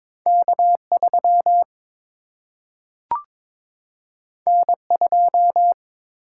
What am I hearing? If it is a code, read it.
Morse code, heard as a keyed tone, about 22 wpm, 700 Hz. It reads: K 3 N 2
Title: Single-Letter Single-Number 22wpm